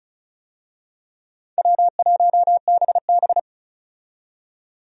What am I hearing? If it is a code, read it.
Morse code, heard as a keyed tone, about 36 wpm, 700 Hz. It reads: W1BB
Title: USA Call Signs - 4 Characters - 35wpm